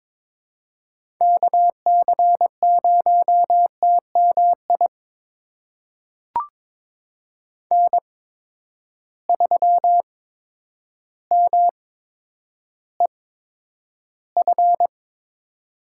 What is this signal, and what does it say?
Morse code, heard as a keyed tone, about 22 wpm, spaced as for 10 wpm, 700 Hz. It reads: KC0TMI N3MEF
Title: USA Call Signs - 8x Character Spacing - 22wpm